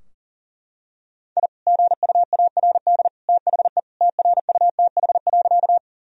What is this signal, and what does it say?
Morse code, heard as a keyed tone, about 40 wpm, 700 Hz. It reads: I GUARD THE TRUTH.
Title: Sentences from Top 1000 Words - Review Part 4 - 40wpm